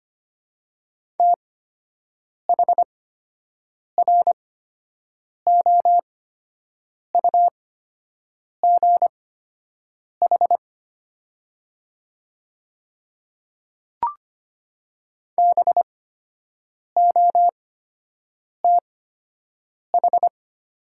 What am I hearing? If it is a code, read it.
Morse code, heard as a keyed tone, about 25 wpm, spaced as for 7 wpm, 690 Hz. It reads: THROUGH BOTH
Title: 8x Character Spacing - Top 500 Words - 25wpm